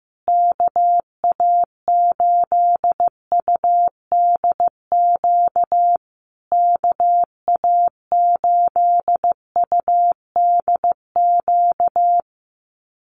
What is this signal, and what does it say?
Morse code, heard as a keyed tone, about 15 wpm, 700 Hz. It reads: KA8UDQ KA8UDQ
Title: USA Call Signs Repeated 3x 15wpm